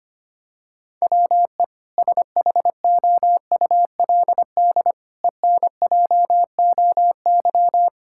Morse code, heard as a keyed tone, about 25 wpm, 700 Hz. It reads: WE SHOULD ENJOY
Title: Sentences from Top 1000 Words - Review Part 5 - 25wpm